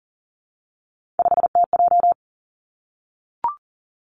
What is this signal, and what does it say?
Morse code, heard as a keyed tone, about 41 wpm, 700 Hz. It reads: 5TJ
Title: Three Challenging Letter-Number Combinations 40wpm